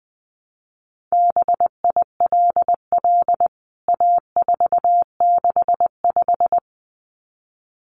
Morse code, heard as a keyed tone, about 20 wpm, 700 Hz. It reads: BILL A465
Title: CWT Exchange 20wpm